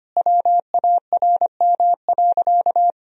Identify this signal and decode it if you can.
Morse code, heard as a keyed tone, about 25 wpm, 700 Hz. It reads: WARM.
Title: Sentences from Top 1000 Words - Review Part 2 - 25wpm